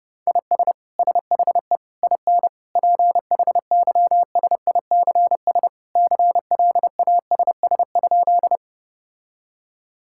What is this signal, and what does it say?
Morse code, heard as a keyed tone, about 30 wpm, 700 Hz. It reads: IS SHE IN PHYSICS CLASS?
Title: Sentences from Top 1000 Words - Review Part 5 - 30wpm